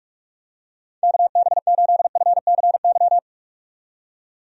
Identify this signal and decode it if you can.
Morse code, heard as a keyed tone, about 47 wpm, 700 Hz. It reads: KB8FCY
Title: USA Call Signs - 6 Characters - 45wpm